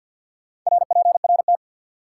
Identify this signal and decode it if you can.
Morse code, heard as a keyed tone, about 47 wpm, 700 Hz. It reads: RPRT
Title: Top 90 QSO Elements 50wpm